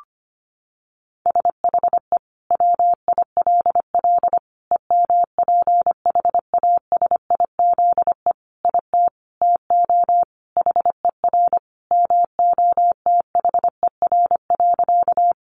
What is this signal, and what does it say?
Morse code, heard as a keyed tone, about 25 wpm, 700 Hz. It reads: SHE WILL EMPHASIZE IT TO HER MOTHER.
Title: Sentences from Top 1500 Words 25wpm